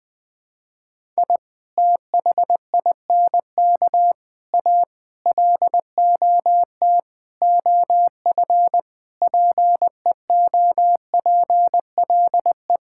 Morse code, heard as a keyed tone, about 21 wpm, 700 Hz. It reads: I THINK A LOT OF PEOPLE
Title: Sentences with 5-Word Phrases - 20wpm